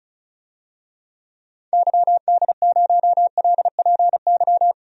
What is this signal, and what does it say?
Morse code, heard as a keyed tone, about 34 wpm, 700 Hz. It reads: YD0LPY
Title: DXCC-100 Call Signs - 6 Characters - 35wpm